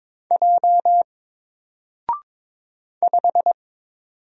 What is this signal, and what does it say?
Morse code, heard as a keyed tone, about 22 wpm, 700 Hz. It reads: J 5
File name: Single-Letter-Number-22wpm